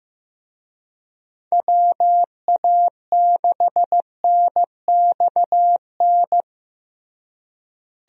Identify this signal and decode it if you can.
Morse code, heard as a keyed tone, about 15 wpm, 700 Hz. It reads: WA6NXN
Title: USA Call Signs - 6 Characters - Repeated 2x - 15wpm